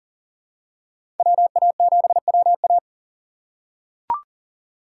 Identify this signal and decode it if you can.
Morse code, heard as a keyed tone, about 40 wpm, 700 Hz. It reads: WA7WA
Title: USA Call Signs - 5 Characters - 3x Character Spacing - 40wpm